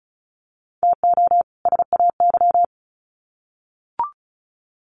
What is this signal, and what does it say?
Morse code, heard as a keyed tone, about 34 wpm, 700 Hz. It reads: TO SAY
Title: Sets of 2 Words 35wpm